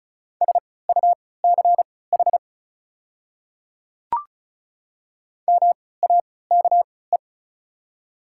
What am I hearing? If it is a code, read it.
Morse code, heard as a keyed tone, about 36 wpm, spaced as for 20 wpm, 690 Hz. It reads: SUCH MAKE
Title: 3x Character Spacing - Top 500 Words - 35wpm